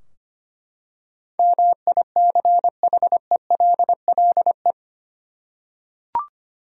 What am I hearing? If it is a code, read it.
Morse code, heard as a keyed tone, about 25 wpm, 700 Hz. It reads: MICHELLE
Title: USA Common Names - 25wpm